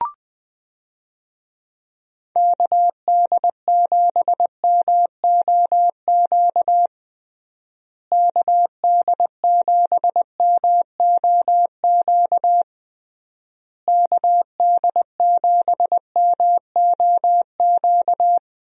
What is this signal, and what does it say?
Morse code, heard as a keyed tone, about 20 wpm, 700 Hz. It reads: KD7MOQ KD7MOQ KD7MOQ
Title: USA Call Signs - 6 Characters - Repeated 3x - 20wpm